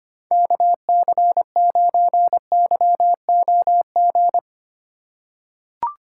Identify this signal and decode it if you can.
Morse code, heard as a keyed tone, about 25 wpm, 700 Hz. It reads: KC9YOG
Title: USA Call Signs - 6 Characters - 6x Character Spacing - 25wpm